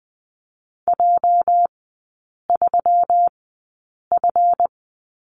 Morse code, heard as a keyed tone, about 20 wpm, spaced as for 8 wpm, 700 Hz. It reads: J3F
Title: 205 Lesson 33e - Letter j - Three Characters 20wpm